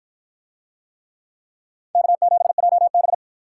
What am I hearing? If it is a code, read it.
Morse code, heard as a keyed tone, about 53 wpm, 700 Hz. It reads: K7JB